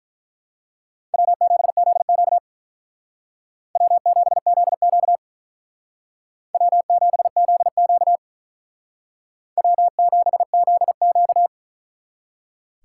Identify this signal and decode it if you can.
Morse code, heard as a keyed tone, about 43 wpm, 700 Hz. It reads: W7ZQ W7ZQ W7ZQ W7ZQ
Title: Speed Racing - USA Call Signs 35wpm